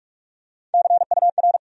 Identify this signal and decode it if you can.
Morse code, heard as a keyed tone, about 46 wpm, 700 Hz. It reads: CUR
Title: USA Call Sign Postfixes - 3 Letters 45wpm